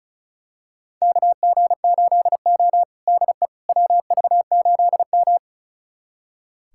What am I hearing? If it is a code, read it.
Morse code, heard as a keyed tone, about 36 wpm, 700 Hz. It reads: KG8O DE WV8M
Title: USA Callsign DE Callsign - 4 Characters - 35wpm